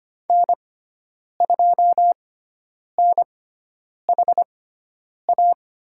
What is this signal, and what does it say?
Morse code, heard as a keyed tone, about 25 wpm, spaced as for 9 wpm, 700 Hz. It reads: N2NHA
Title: USA Call Signs - 5 Characters - 6x Character Spacing - 25wpm